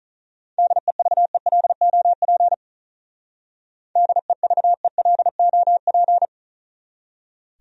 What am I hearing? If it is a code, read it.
Morse code, heard as a keyed tone, about 42 wpm, 700 Hz. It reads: DEVELOP DEVELOP
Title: Speed-Racing Top 500 Words 35wpm